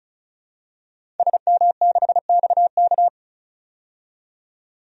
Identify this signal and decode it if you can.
Morse code, heard as a keyed tone, about 34 wpm, 700 Hz. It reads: SM6XK
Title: DXCC-100 Call Signs - 5 Characters - 35wpm